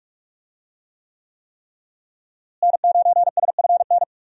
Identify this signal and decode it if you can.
Morse code, heard as a keyed tone, about 42 wpm, 700 Hz. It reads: N9SFN
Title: USA Call Signs - 5 Characters - Repeated 3x - 45wpm